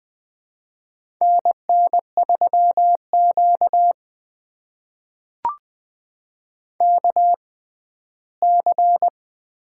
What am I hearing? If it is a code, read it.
Morse code, heard as a keyed tone, about 20 wpm, spaced as for 14 wpm, 700 Hz. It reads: NN3Q KC
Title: USA Call Signs - 4 Characters - 6x Character Spacing - 20wpm